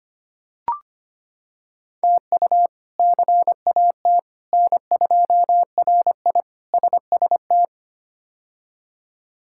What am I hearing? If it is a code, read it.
Morse code, heard as a keyed tone, about 25 wpm, 700 Hz. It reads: TU CAT N2RI SST
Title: K1USN SST QSO Practice - 25wpm